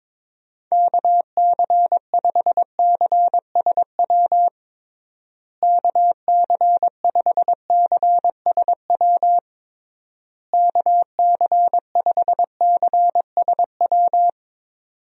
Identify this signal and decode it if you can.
Morse code, heard as a keyed tone, about 22 wpm, 700 Hz. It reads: KC5CSW KC5CSW KC5CSW
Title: USA Call Signs - 6 Characters - Repeated 3x - 22wpm